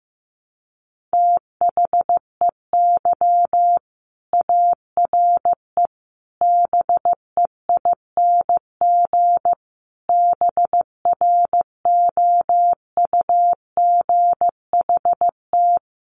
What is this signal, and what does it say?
Morse code, heard as a keyed tone, about 15 wpm, 700 Hz. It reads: THEY ARE BEING BROUGHT
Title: Sentences from Top 600 Words 15wpm